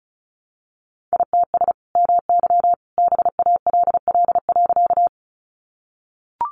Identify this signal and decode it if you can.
Morse code, heard as a keyed tone, about 34 wpm, 700 Hz. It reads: ITS MY BALL.
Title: Sentences from Top 900 Words 35wpm